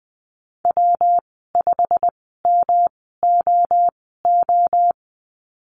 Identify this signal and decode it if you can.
Morse code, heard as a keyed tone, about 20 wpm, spaced as for 14 wpm, 700 Hz. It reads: W5MOO
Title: 219 Lesson 35g - Number 0 - Call Signs 20wpm